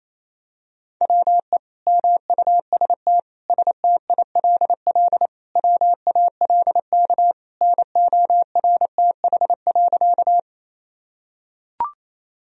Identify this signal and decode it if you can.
Morse code, heard as a keyed tone, about 28 wpm, 700 Hz. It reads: WE MUST STILL WALK NORTH.
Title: Sentences from Top 1000 Words - Review Part 1 - 28wpm